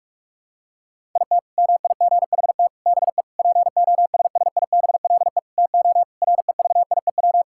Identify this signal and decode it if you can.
Morse code, heard as a keyed tone, about 46 wpm, 700 Hz. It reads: IT MIGHT BE POSSIBLE TO REVIEW
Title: Sentences from Top 1000 Words - Review Part 6 - 45wpm